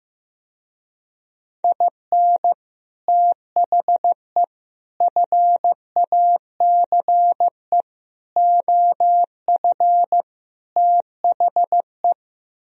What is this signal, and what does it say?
Morse code, heard as a keyed tone, about 15 wpm, 700 Hz. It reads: IN THE FACE OF THE
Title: Speed-Racing - Sets of 5 Words 15wpm